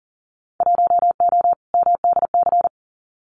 Morse code, heard as a keyed tone, about 38 wpm, 700 Hz. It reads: 1O MDC
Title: Winter Field Day Exchanges 40wpm